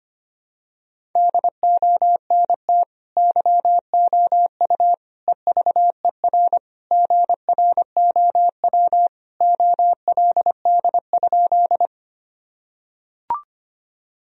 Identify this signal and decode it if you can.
Morse code, heard as a keyed tone, about 25 wpm, 700 Hz. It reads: DONT YOU EVER GROW OLD?
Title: Sentences from Top 1000 Words - Review Part 1 - 25wpm